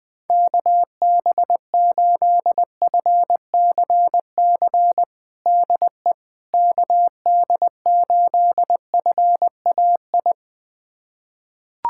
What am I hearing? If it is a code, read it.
Morse code, heard as a keyed tone, about 20 wpm, 700 Hz. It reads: KB8FCC DE KD8FAI
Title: USA Callsign DE Callsign 20wpm